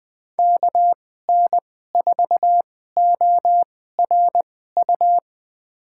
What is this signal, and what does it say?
Morse code, heard as a keyed tone, about 20 wpm, spaced as for 14 wpm, 700 Hz. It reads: KN4ORU
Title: USA Call Signs - 6 Characters - 2x Character Spacing - 20wpm